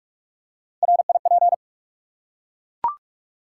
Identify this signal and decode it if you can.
Morse code, heard as a keyed tone, about 44 wpm, 700 Hz. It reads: RIP
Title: Three Letter Words - 45wpm